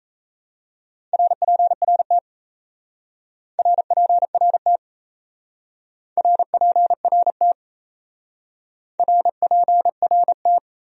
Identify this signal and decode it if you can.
Morse code, heard as a keyed tone, about 36 wpm, 700 Hz. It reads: RPRT RPRT RPRT RPRT